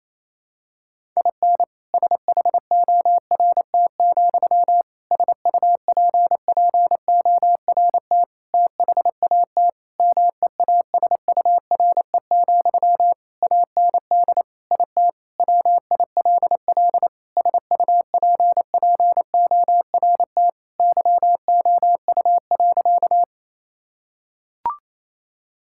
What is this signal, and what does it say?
Morse code, heard as a keyed tone, about 27 wpm, 700 Hz. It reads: IN SHORT, SUPPORT THAT MEASURE, AND IT WILL SUPPORT YOU.
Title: Sentences from Top 1000 Words - Review Part 5 - 28wpm